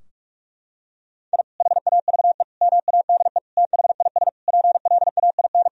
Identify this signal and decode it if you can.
Morse code, heard as a keyed tone, about 46 wpm, 700 Hz. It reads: I HAVE MADE THIS PLAIN
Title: Sentences from Top 1000 Words - Review Part 2 - 45wpm